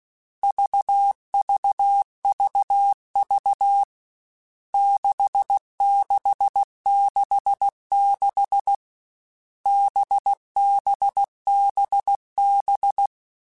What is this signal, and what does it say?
Morse code, heard as a keyed tone, about 16 wpm, 800 Hz. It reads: VVVV 6666 BBBB
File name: Groups@13 / Groups@13 mp3s / VB46@13